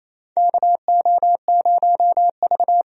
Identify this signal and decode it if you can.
Morse code, heard as a keyed tone, about 28 wpm, 700 Hz. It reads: KO0V
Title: USA Call Signs - 4 Characters - 28wpm